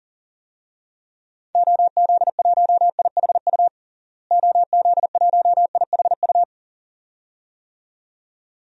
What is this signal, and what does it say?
Morse code, heard as a keyed tone, about 40 wpm, 700 Hz. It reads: OZ1IHU OZ1IHU
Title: DXCC-100 Call Signs - 6 Characters - Repeated 2x - 40wpm